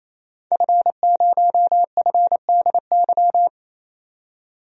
Morse code, heard as a keyed tone, about 28 wpm, 700 Hz. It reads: F0FDY
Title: DXCC-100 Call Signs - 5 Characters - 28wpm